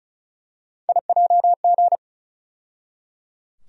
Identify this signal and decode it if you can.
Morse code, heard as a keyed tone, about 36 wpm, 700 Hz. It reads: IJG
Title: USA Call Sign Postfixes - 3 Letters 35wpm